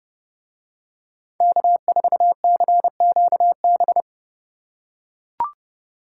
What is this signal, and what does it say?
Morse code, heard as a keyed tone, about 28 wpm, 700 Hz. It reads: K4CQB